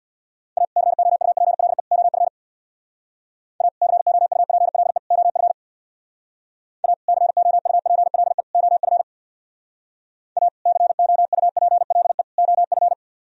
Morse code, heard as a keyed tone, about 64 wpm, 700 Hz. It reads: A COUPLE OF A COUPLE OF A COUPLE OF A COUPLE OF